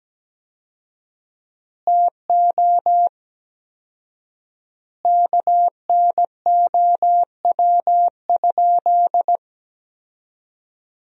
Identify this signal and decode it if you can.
Morse code, heard as a keyed tone, about 17 wpm, 700 Hz. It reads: TO KNOW?
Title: 4x Word-Spacing - Sentences from Top 200 Words - 17wpm